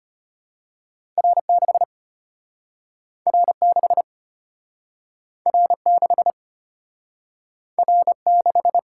Morse code, heard as a keyed tone, about 34 wpm, 700 Hz. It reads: R6 R6 R6 R6